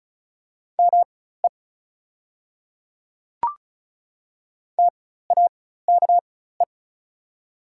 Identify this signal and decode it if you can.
Morse code, heard as a keyed tone, about 36 wpm, spaced as for 16 wpm, 700 Hz. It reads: ME TAKE
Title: 4x Character Spacing - Top 500 Words - 35wpm